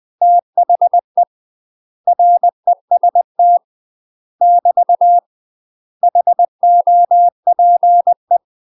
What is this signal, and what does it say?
Morse code, heard as a keyed tone, about 20 wpm, 700 Hz. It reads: THE REST <BT> HOPE
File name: UK MASTER QSOs TRAINING / 20WPM / UK MASTER QSOs 0030